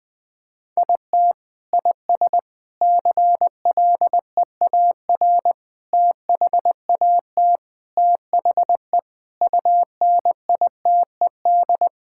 Morse code, heard as a keyed tone, about 21 wpm, 700 Hz. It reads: IT IS CLEAR THAT THE UNITED
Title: Sentences with 4-Word Phrases - Spoken Beginning - 20wpm